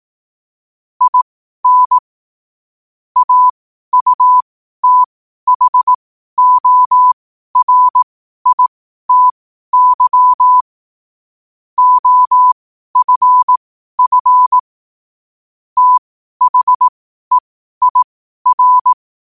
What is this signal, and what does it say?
Morse code, heard as a keyed tone, about 18 wpm, spaced as for 12 wpm, 1000 Hz. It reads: IN AUTHORITY OFF THEIR
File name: qotdfiles / 12wpm